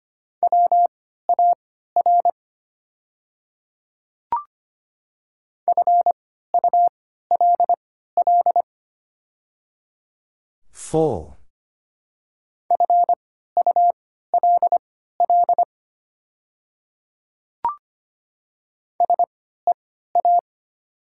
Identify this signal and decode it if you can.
Morse code, heard as a keyed tone, about 25 wpm, spaced as for 14 wpm, 700 Hz. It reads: WAR FULL FULL SEA
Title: 3x Character Spacing - Top 500 Words - 25wpm